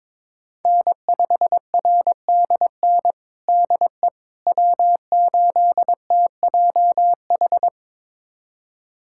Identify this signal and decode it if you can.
Morse code, heard as a keyed tone, about 22 wpm, 700 Hz. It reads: N5RDN DE W8TJH
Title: USA Callsign DE Callsign - 5 Characters - 22wpm